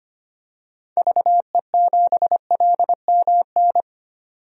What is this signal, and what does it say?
Morse code, heard as a keyed tone, about 25 wpm, 700 Hz. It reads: VE7LMN
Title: DXCC-100 Call Signs - 6 Characters - 25wpm